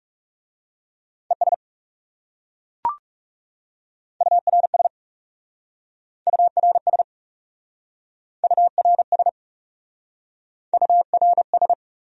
Morse code, heard as a keyed tone, about 40 wpm, 700 Hz. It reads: ES URS URS URS URS